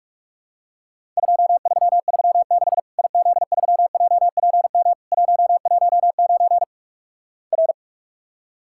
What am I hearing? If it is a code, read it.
Morse code, heard as a keyed tone, about 46 wpm, 700 Hz. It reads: 2336 IZ3JPM 119 R
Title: Worked All Europe DX Contest - QTC Exchange - 45wpm